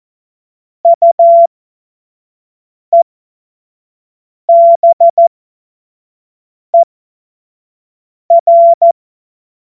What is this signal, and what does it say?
Morse code, heard as a keyed tone, about 14 wpm, spaced as for 5 wpm, 680 Hz. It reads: UEBER